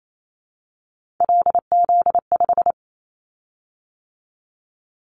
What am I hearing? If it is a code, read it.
Morse code, heard as a keyed tone, about 28 wpm, 700 Hz. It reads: LZ5
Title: Callsign Prefixes for Top 25 DXCC - 28wpm